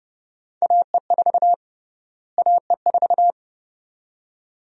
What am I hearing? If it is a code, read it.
Morse code, heard as a keyed tone, about 29 wpm, 700 Hz. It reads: AE4 AE4
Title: USA Call Sign Prefixes - Repeated 2x - 30wpm